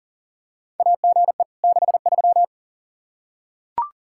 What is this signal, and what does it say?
Morse code, heard as a keyed tone, about 40 wpm, 700 Hz. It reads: AGE 63
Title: Practice QSO - Exchange 3 - More Info - 40wpm